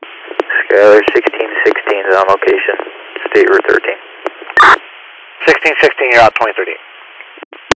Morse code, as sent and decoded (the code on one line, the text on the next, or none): none